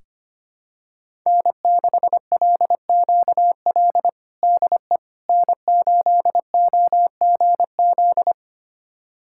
Morse code, heard as a keyed tone, about 25 wpm, 700 Hz. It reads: N6LQL DE N8OGZ
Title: USA Callsign DE Callsign - 5 Characters - 25wpm